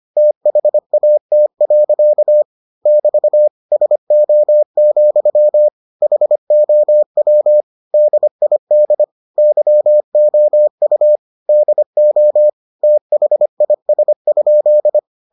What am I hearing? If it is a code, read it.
Morse code, heard as a keyed tone, about 25 wpm, 600 Hz. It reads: THAT. <BT> SO, HOW DID YOU DO THIS?